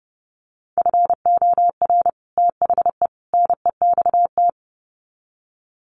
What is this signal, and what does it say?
Morse code, heard as a keyed tone, about 30 wpm, 700 Hz. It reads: FOR THE NEXT
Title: Sets of 3 Words 30wpm